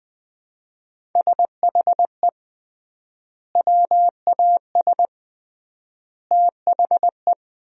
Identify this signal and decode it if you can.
Morse code, heard as a keyed tone, about 20 wpm, 700 Hz. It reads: SHE WAS THE
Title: 3x Word-Spacing - Sentences from Top 200 Words - 20wpm